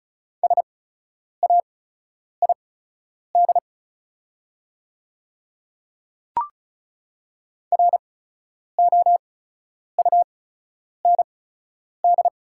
Morse code, heard as a keyed tone, about 36 wpm, spaced as for 10 wpm, 700 Hz. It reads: SAID ROUND
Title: 8x Character Spacing - Top 300 Words - 35wpm